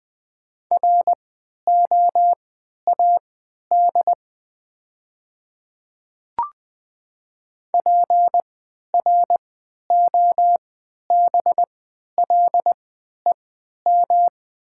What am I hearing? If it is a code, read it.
Morse code, heard as a keyed tone, about 20 wpm, spaced as for 11 wpm, 700 Hz. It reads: ROAD PROBLEM
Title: 3x Character Spacing - Top 500 Words - 20wpm